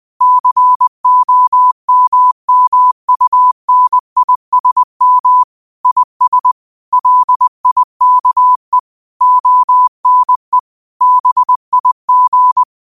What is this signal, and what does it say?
Morse code, heard as a keyed tone, about 20 wpm, 1000 Hz. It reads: COMMUNISM IS LIKE ONE BIG